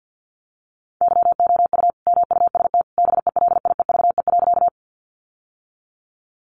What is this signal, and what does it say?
Morse code, heard as a keyed tone, about 49 wpm, 700 Hz. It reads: YOU MUST BELIEVE.
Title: Sentences from Top 1000 Words 50wpm